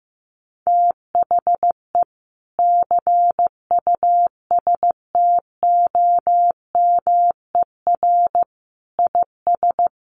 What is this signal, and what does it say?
Morse code, heard as a keyed tone, about 15 wpm, 700 Hz. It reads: THE CUSTOMER IS
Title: Sentences from Top 1100 Words 15wpm